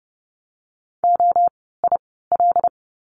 Morse code, heard as a keyed tone, about 31 wpm, spaced as for 17 wpm, 700 Hz. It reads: OIL
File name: Lesson 17f - Letter u - Words 30wpm